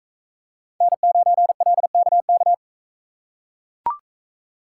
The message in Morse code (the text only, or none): N9LKK